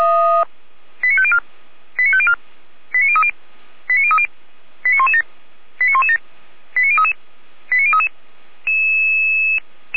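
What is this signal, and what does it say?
Five-tone selective call (ZVEI), heard as quick runs of five tones, tones of about 70 ms. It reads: 89504 89504 89030 89030 89117 89117 89033 89033
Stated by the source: ZVEI five-tone selective call received with an SDR.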